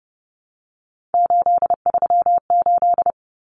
Morse code, heard as a keyed tone, about 30 wpm, 700 Hz. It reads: 838
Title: Numbers - 3-Digit 30wpm